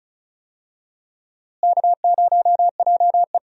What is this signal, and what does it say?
Morse code, heard as a keyed tone, about 36 wpm, 700 Hz. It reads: K0JE
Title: USA Call Signs - 4 Characters - Repeated 2x - 35wpm